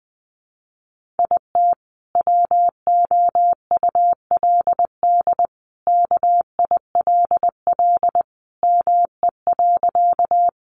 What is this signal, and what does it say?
Morse code, heard as a keyed tone, about 20 wpm, 700 Hz. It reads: IT WOULD KILL ME.
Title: Sentences from Top 1400 Words 20wpm